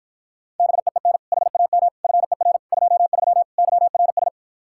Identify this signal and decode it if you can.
Morse code, heard as a keyed tone, about 52 wpm, 700 Hz. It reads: BEEN HAM FER 23 YRS